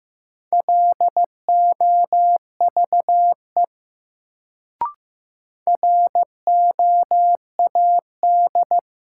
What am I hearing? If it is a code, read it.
Morse code, heard as a keyed tone, about 15 wpm, 700 Hz. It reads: LOVE ROAD
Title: Sentences from Top 400 Words - New Word Drill - 15wpm